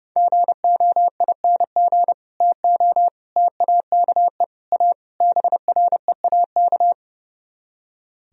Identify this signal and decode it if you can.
Morse code, heard as a keyed tone, about 38 wpm, 700 Hz. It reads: GOING TO TAKE A BREAK